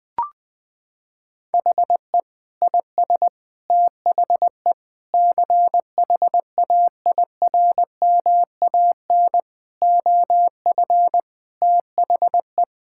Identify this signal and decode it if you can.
Morse code, heard as a keyed tone, about 20 wpm, 700 Hz. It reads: HE IS THE CHAIRMAN OF THE
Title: Sentences with 3-Word Phrases - 20wpm